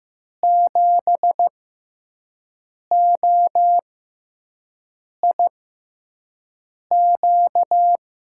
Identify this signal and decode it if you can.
Morse code, heard as a keyed tone, about 15 wpm, spaced as for 5 wpm, 700 Hz. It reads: 7OIQ
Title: USA Call Signs - 6 Characters - 6x Character Spacing - 15wpm